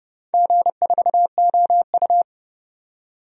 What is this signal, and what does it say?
Morse code, heard as a keyed tone, about 30 wpm, 700 Hz. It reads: G4OU
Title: DXCC-100 Call Signs - 4 Characters - 30wpm